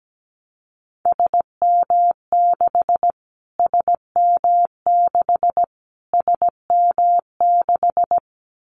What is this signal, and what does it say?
Morse code, heard as a keyed tone, about 17 wpm, 700 Hz. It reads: SM6 SM6 SM6
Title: Callsign Prefixes for Top 25 DXCC - Repeated 3x - 17wpm